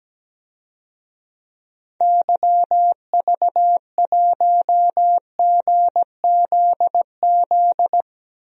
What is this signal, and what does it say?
Morse code, heard as a keyed tone, about 17 wpm, 700 Hz. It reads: YV1GZZ
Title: DXCC-100 Call Signs - 6 Characters - Repeated 2x - 17wpm